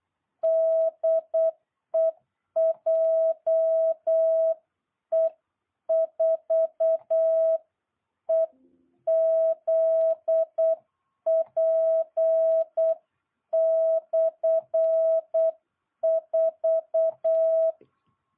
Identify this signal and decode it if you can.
Morse code, heard as a keyed tone, about 8 wpm, 660 Hz. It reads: DE JE4EZP/4